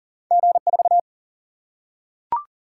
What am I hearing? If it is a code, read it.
Morse code, heard as a keyed tone, about 38 wpm, 700 Hz. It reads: G4
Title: Single-Letter Single-Number 40wpm